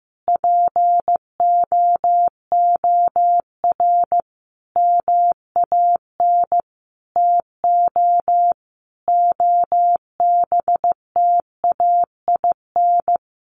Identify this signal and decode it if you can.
Morse code, heard as a keyed tone, about 15 wpm, 700 Hz. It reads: POOR MAN TO OBTAIN
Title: Sentences from Top 1300 Words 15wpm